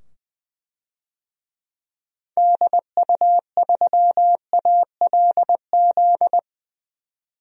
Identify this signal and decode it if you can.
Morse code, heard as a keyed tone, about 20 wpm, 700 Hz. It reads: DU3ALZ
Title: DXCC-100 Call Signs - 6 Characters - 20wpm